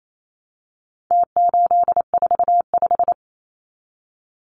Morse code, heard as a keyed tone, about 28 wpm, 700 Hz. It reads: T845
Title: Numbers - Contest Style - 4-Digit 28wpm